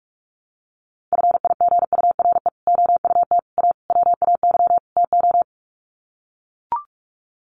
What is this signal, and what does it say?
Morse code, heard as a keyed tone, about 44 wpm, 700 Hz. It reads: FIGURE OUT A WAY TO
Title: Sets of 5 Words Encore2 45wpm